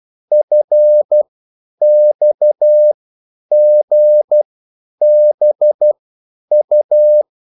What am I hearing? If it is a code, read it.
Morse code, heard as a keyed tone, about 12 wpm, spaced as for 9 wpm, 600 Hz. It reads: FXGBU